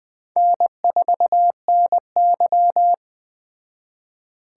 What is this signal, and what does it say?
Morse code, heard as a keyed tone, about 20 wpm, 700 Hz. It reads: N4NY
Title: Speed Racing - USA 1x2 Call Signs 17wpm